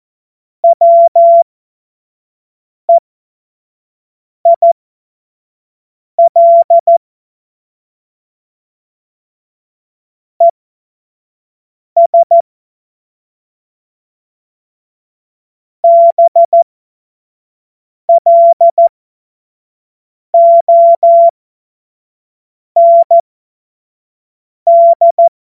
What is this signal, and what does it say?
Morse code, heard as a keyed tone, about 14 wpm, spaced as for 5 wpm, 680 Hz. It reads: WEIL ES BLOND
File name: Morsekurs / Witze / Witz 16. November 2021, 25 F70